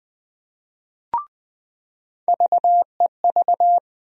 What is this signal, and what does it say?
Morse code, heard as a keyed tone, about 20 wpm, 700 Hz. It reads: VEV
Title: USA Call Sign Postfixes - 3 Letters 20wpm